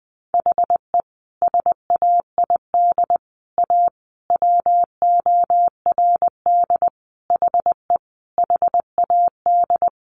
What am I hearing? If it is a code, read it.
Morse code, heard as a keyed tone, about 20 wpm, 700 Hz. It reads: HE SAID A WORD HE HAD
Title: Sentences from Top 600 Words 20wpm